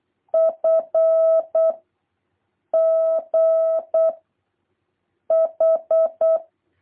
Morse code, heard as a keyed tone, about 8 wpm, 650 Hz. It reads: F G H
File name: jh8cbh / 2023cw